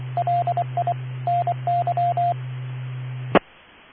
Morse code, heard as a keyed tone, about 25 wpm, 700 Hz. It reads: LI NY E